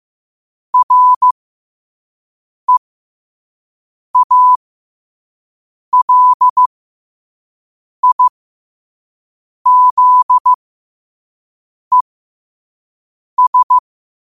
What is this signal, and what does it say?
Morse code, heard as a keyed tone, about 15 wpm, spaced as for 5 wpm, 1000 Hz. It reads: REALIZES
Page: learn-to-code